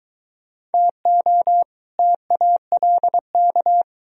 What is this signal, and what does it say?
Morse code, heard as a keyed tone, about 22 wpm, 700 Hz. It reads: TO TALK